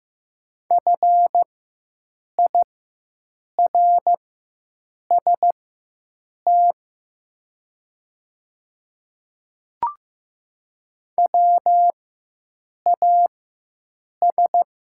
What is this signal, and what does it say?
Morse code, heard as a keyed tone, about 15 wpm, spaced as for 7 wpm, 700 Hz. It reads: FIRST WAS
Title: 4x Character Spacing - Top 100 Words - 15wpm